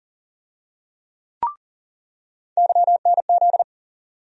Morse code, heard as a keyed tone, about 40 wpm, 700 Hz. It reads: YNZ
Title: USA Call Sign Postfixes - 3 Letters 40wpm